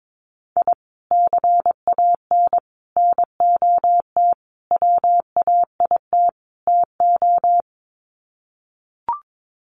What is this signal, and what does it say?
Morse code, heard as a keyed tone, about 22 wpm, 700 Hz. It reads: I CAN NOT WAIT TO
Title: Sets of 5 Words Encore 22wpm